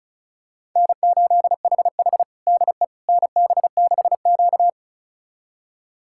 Morse code, heard as a keyed tone, about 34 wpm, 700 Hz. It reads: N8HH DE NB6Q
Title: USA Callsign DE Callsign - 4 Characters - 35wpm